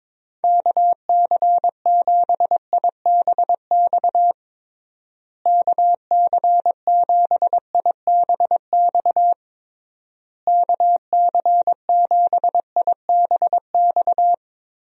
Morse code, heard as a keyed tone, about 22 wpm, 700 Hz. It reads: KC7IBX KC7IBX KC7IBX
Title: USA Call Signs - 6 Characters - Repeated 3x - 22wpm